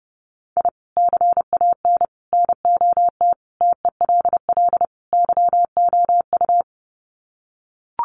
Morse code, heard as a keyed tone, about 30 wpm, 700 Hz. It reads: I CAN NOT TELL YOU
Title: Sets of 5 Words 30wpm